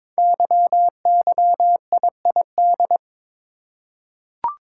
Morse code, heard as a keyed tone, about 22 wpm, 700 Hz. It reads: YYIID
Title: Code Groups - 5 Letter:Numbers - 22wpm